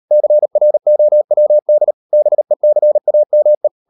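Morse code, heard as a keyed tone, about 38 wpm, 600 Hz. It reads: CROWD BECAME
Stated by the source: Morse code practice at 38 WPM, sending Ken Follett's novel Never in English, from CW med Gnister.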